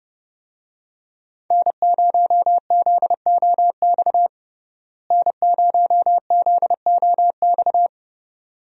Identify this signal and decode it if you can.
Morse code, heard as a keyed tone, about 31 wpm, 700 Hz. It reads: N0ZOX N0ZOX
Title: USA Call Signs - 5 Characters - Repeated 2x - 30wpm